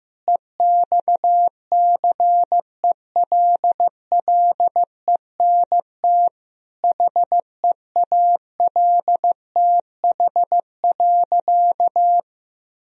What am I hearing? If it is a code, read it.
Morse code, heard as a keyed tone, about 16 wpm, 700 Hz. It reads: EXCELLENT HEALTH.
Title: Sentences from Top 1000 Words - Review Part 3 - 15wpm